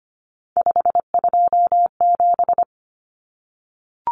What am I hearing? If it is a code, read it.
Morse code, heard as a keyed tone, about 25 wpm, 700 Hz. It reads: 527
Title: Numbers - 3-Digit 25wpm